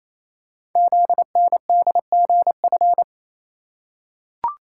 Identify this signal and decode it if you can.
Morse code, heard as a keyed tone, about 28 wpm, 700 Hz. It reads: ZNDGF
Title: Code Groups - 5 Letters - 28wpm